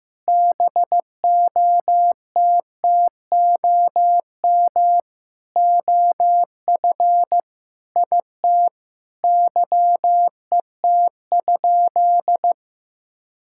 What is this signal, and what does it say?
Morse code, heard as a keyed tone, about 15 wpm, 700 Hz. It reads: BOTTOM OF IT YET?
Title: Sentences from Top 1000 Words - Review Part 6 - 15wpm